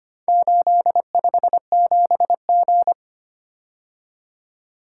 Morse code, heard as a keyed tone, about 25 wpm, 700 Hz. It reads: 857G
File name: practice / 25wpm / Four Letter-Number Combinations - 25wpm